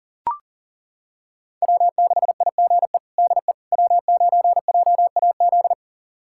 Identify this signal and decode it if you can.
Morse code, heard as a keyed tone, about 40 wpm, 700 Hz. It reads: W6IGE DE W9JAZ
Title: USA Callsign DE Callsign - 5 Characters - 40wpm